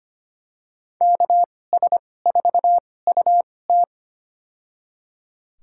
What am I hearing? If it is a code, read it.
Morse code, heard as a keyed tone, about 25 wpm, spaced as for 18 wpm, 700 Hz. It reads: KS4UT
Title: USA Call Signs - 5 Characters - 2x Character Spacing - 25wpm